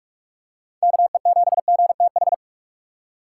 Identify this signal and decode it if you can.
Morse code, heard as a keyed tone, about 46 wpm, 700 Hz. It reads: KE7GTH
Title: USA Call Signs - 6 Characters - 3x Character Spacing - 45wpm